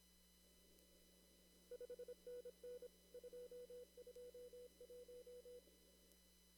Morse code, heard as a keyed tone, about 26 wpm, 490 Hz. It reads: 5NN 221
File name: CW / kt5j wpx cw-op-k5pi